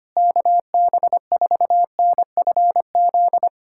Morse code, heard as a keyed tone, about 25 wpm, 700 Hz. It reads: KB4NFZ
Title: USA Callsign DE Callsign 25wpm